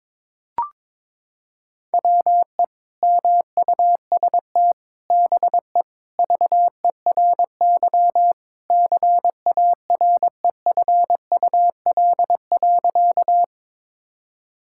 Morse code, heard as a keyed tone, about 22 wpm, 700 Hz. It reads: WE MUST BE VERY CAREFUL.
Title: Sentences from Top 1600 Words 22wpm